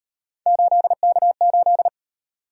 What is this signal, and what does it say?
Morse code, heard as a keyed tone, about 34 wpm, 700 Hz. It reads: 8K8